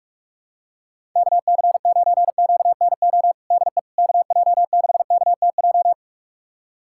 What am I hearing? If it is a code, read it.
Morse code, heard as a keyed tone, about 47 wpm, 700 Hz. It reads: KC9QNO DE KJ6KTJ
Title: USA Callsign DE Callsign 45wpm